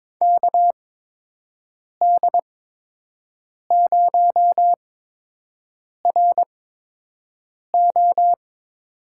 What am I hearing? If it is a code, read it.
Morse code, heard as a keyed tone, about 22 wpm, spaced as for 6 wpm, 700 Hz. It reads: KD0RO
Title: USA Call Signs - 5 Characters - 8x Character Spacing - 22wpm